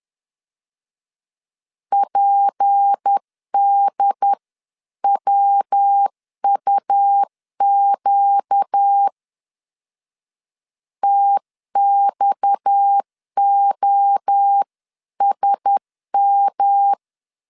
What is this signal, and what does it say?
Morse code, heard as a keyed tone, about 10 wpm, 800 Hz. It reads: PDWUQ TXOSM
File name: CW / DET / 1 6 1500C S C 10wpm 4252